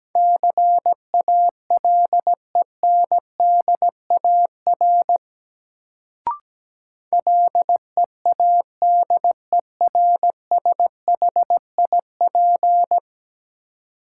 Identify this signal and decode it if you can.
Morse code, heard as a keyed tone, about 17 wpm, 700 Hz. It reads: CALENDAR LEADERSHIP
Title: Sentences from Top 1600 Words - New Word Drill - 17wpm